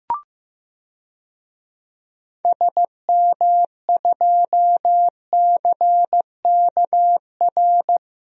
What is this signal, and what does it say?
Morse code, heard as a keyed tone, about 15 wpm, 700 Hz. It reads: SM2CKR
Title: DXCC-100 Call Signs - 6 Characters - 15wpm